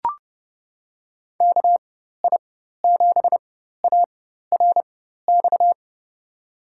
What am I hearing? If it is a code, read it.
Morse code, heard as a keyed tone, about 29 wpm, spaced as for 14 wpm, 700 Hz. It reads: KI7ARX
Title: USA Call Signs - 6 Characters - 4x Character Spacing - 30wpm